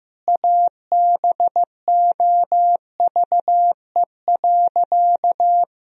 Morse code, heard as a keyed tone, about 15 wpm, 700 Hz. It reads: ABOVE.
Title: Sentences from Top 1000 Words - Review Part 2 - 15wpm